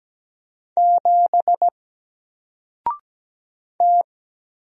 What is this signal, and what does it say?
Morse code, heard as a keyed tone, about 17 wpm, 700 Hz. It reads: 7 T
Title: Single Letter-Number Alt 17wpm